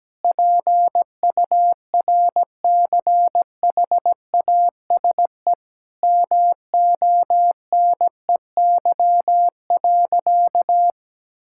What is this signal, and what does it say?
Morse code, heard as a keyed tone, about 17 wpm, 700 Hz. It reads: PURCHASE MONEY.
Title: Sentences from Top 1000 Words - Review Part 5 - 17wpm